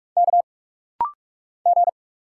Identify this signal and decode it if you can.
Morse code, heard as a keyed tone, about 46 wpm, 700 Hz. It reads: K G
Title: Single Letters - Rapid-Fire 45wpm